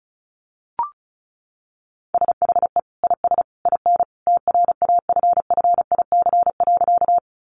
Morse code, heard as a keyed tone, about 34 wpm, 700 Hz. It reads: SHE IS IN TRAFFIC.
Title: Sentences from Top 1300 Words 35wpm